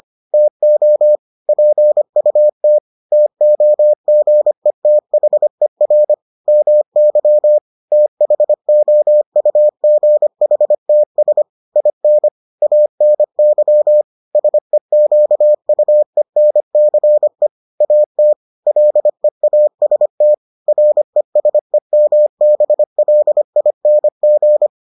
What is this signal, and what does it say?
Morse code, heard as a keyed tone, about 25 wpm, 600 Hz. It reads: TO PUT TOGETHER MY THOUGHTS IN ANY SEQUENCE AT LEAST RESEMBLING